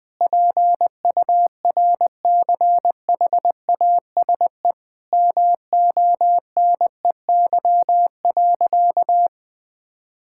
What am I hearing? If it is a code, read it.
Morse code, heard as a keyed tone, about 21 wpm, 700 Hz. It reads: PURCHASE MONEY.
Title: Sentences from Top 1000 Words - Review Part 5 - 20wpm